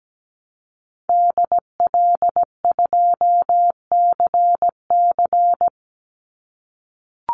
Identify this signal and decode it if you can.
Morse code, heard as a keyed tone, about 17 wpm, 700 Hz. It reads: DL2CC
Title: Call Signs - 17wpm